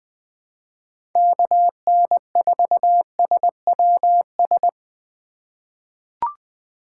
Morse code, heard as a keyed tone, about 20 wpm, 700 Hz. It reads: KN4SWS
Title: 2023 POTA Activator Call Signs - 20wpm